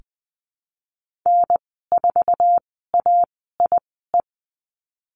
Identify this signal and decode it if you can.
Morse code, heard as a keyed tone, about 20 wpm, spaced as for 15 wpm, 700 Hz. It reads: N4AIE
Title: 055 Lesson 10g - Letter r - Call Signs 20wpm